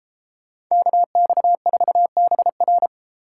Morse code, heard as a keyed tone, about 33 wpm, 700 Hz. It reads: KX4BR
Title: Speed Racing - USA Call Signs 28wpm